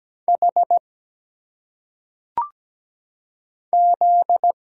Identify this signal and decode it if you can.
Morse code, heard as a keyed tone, about 17 wpm, 700 Hz. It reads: H Z
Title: Single Letters - Phonetic Pronunciation 17wpm